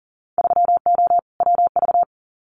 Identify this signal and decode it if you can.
Morse code, heard as a keyed tone, about 37 wpm, 700 Hz. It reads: 3O WV
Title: Winter Field Day Exchanges 40wpm